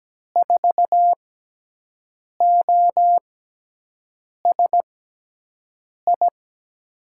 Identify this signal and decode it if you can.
Morse code, heard as a keyed tone, about 17 wpm, spaced as for 6 wpm, 700 Hz. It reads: 4OSI
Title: USA Call Signs - 6 Characters - 6x Character Spacing - 17wpm